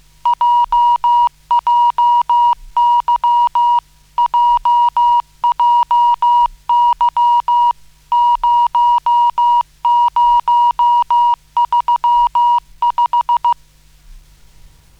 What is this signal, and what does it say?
Morse code, heard as a keyed tone, about 15 wpm, 960 Hz. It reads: JJY JJY 0035